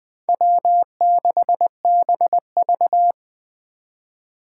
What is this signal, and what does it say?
Morse code, heard as a keyed tone, about 20 wpm, 700 Hz. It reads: W6BV
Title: Four Challenging Letter-Number Combinations 20wpm